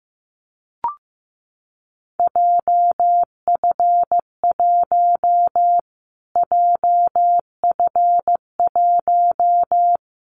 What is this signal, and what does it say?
Morse code, heard as a keyed tone, about 14 wpm, 700 Hz. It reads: JF1 JF1
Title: Callsign Prefixes for Top 25 DXCC - Repeated 3x - 15wpm